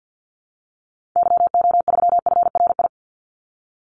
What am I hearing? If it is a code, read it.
Morse code, heard as a keyed tone, about 48 wpm, 700 Hz. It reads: YO3FRI
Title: Call Signs Repeated 2x - 50wpm